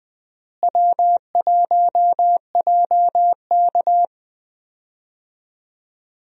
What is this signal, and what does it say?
Morse code, heard as a keyed tone, about 20 wpm, 700 Hz. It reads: W1JK
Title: USA Call Signs - 4 Characters - Repeated 3x - 20wpm